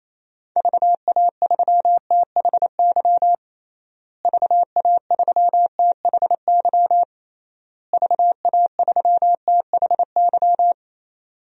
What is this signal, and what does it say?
Morse code, heard as a keyed tone, about 28 wpm, 700 Hz. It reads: VA3THY VA3THY VA3THY
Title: DXCC-100 Call Signs - 6 Characters - Repeated 3x - 28wpm